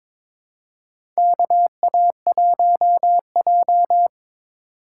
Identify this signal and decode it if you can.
Morse code, heard as a keyed tone, about 21 wpm, 700 Hz. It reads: KA1J
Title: USA Call Signs - 4x Character Spacing - 22wpm